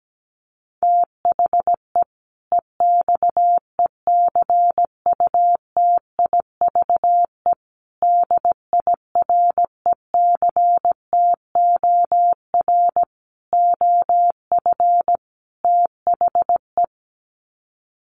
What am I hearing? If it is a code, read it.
Morse code, heard as a keyed tone, about 17 wpm, 700 Hz. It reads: THE EXECUTIVE DIRECTOR OF THE
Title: Sets of 5 Words Encore 17wpm